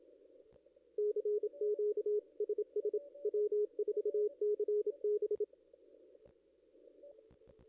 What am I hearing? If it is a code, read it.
Morse code, heard as a keyed tone, about 27 wpm, 420 Hz, with radio noise. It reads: CQ SS W4CB
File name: contests / ss-cw-k5tr-op-n5rz -1700